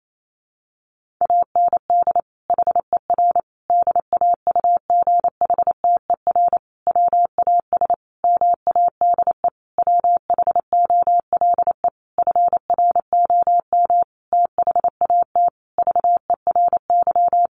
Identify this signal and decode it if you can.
Morse code, heard as a keyed tone, about 28 wpm, 700 Hz. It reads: AND HER DAUGHTER WAS MADE WHOLE FROM THAT VERY
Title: Sentences from Top 1500 Words 28wpm